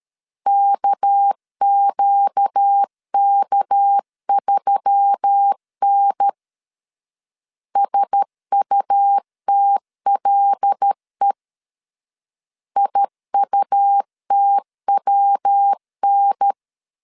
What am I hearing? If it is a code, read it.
Morse code, heard as a keyed tone, about 13 wpm, 800 Hz. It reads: KQK3N SUTLE IUTWN
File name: LEZIONI / 6 PKX0 / 6 PK 12wpm 2006